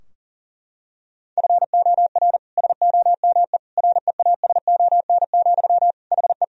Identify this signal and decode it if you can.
Morse code, heard as a keyed tone, about 40 wpm, 700 Hz. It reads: FOR SOME REASON, HE
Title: Sentences from Top 1000 Words - Review Part 3 - 40wpm